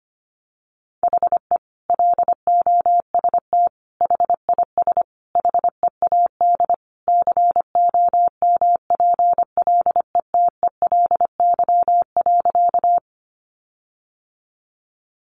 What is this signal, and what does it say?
Morse code, heard as a keyed tone, about 25 wpm, 700 Hz. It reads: HE LOST HIS HEAD COMPLETELY.
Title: Sentences from Top 1400 Words 25wpm